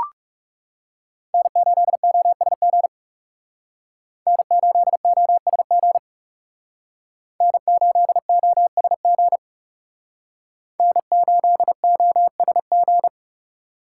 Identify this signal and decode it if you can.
Morse code, heard as a keyed tone, about 36 wpm, 700 Hz. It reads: N8OSG N8OSG N8OSG N8OSG
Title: Speed Racing - USA Call Signs 30wpm